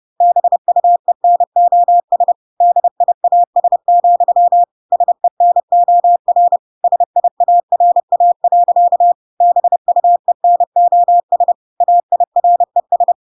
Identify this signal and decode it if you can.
Morse code, heard as a keyed tone, about 30 wpm, 700 Hz. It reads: BUENOS DIAS, SENOR SIARA. BUENOS AIRES